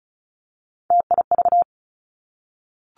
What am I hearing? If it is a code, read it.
Morse code, heard as a keyed tone, about 36 wpm, 700 Hz. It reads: TIV
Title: Three Challenging Letter-Number Combinations 35wpm